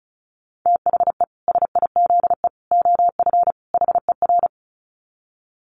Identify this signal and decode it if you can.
Morse code, heard as a keyed tone, about 34 wpm, 700 Hz. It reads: THE SIZE OF HER
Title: Dit Challenge 35wpm